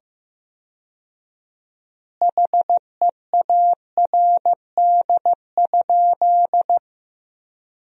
Morse code, heard as a keyed tone, about 15 wpm, 700 Hz. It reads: HEARD?
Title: 4x Word-Spacing - Sentences from Top 400 Words - 15wpm